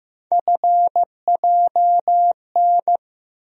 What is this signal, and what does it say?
Morse code, heard as a keyed tone, about 15 wpm, 700 Hz. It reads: FJN
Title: USA Call Sign Postfixes - 3 Letters 15wpm